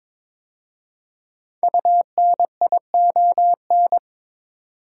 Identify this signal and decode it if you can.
Morse code, heard as a keyed tone, about 22 wpm, 700 Hz. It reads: UNION
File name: 5 Letter Words - First 3 Letters Given - 22wpm